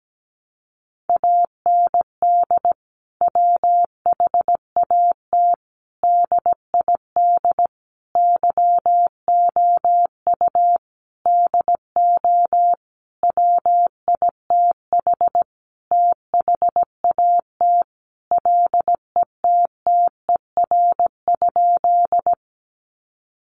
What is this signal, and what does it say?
Morse code, heard as a keyed tone, about 17 wpm, 700 Hz. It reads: AND WHAT DID YOU DO WITH THAT LETTER?
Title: Sentences from Top 300 Words 17wpm